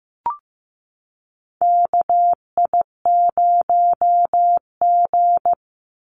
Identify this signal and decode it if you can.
Morse code, heard as a keyed tone, about 15 wpm, 700 Hz. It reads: KI0G
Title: NA SOTA Chaser Callsigns 15wpm